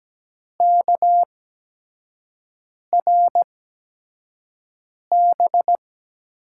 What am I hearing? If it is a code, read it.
Morse code, heard as a keyed tone, about 17 wpm, spaced as for 5 wpm, 700 Hz. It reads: KRB
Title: USA Call Signs - 6 Characters - 8x Character Spacing - 17wpm